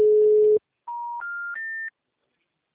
Special information tone.
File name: out- -2005- - - .42108